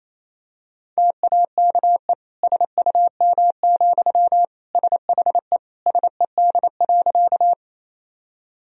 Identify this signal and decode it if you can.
Morse code, heard as a keyed tone, about 28 wpm, 700 Hz. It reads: TAKE SUM, SHE SED.
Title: Sentences from Top 200 Words and Abbreviations 28wpm